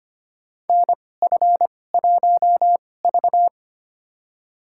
Morse code, heard as a keyed tone, about 25 wpm, spaced as for 18 wpm, 700 Hz. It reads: NF1V
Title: USA Call Signs - 2x Character Spacing - 25wpm